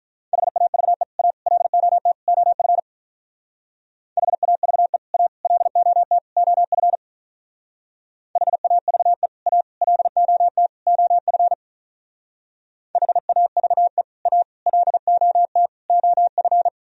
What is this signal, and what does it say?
Morse code, heard as a keyed tone, about 43 wpm, 700 Hz. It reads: HAVE A LOT OF HAVE A LOT OF HAVE A LOT OF HAVE A LOT OF
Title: Speed-Racing - Sets of 4 Words 35wpm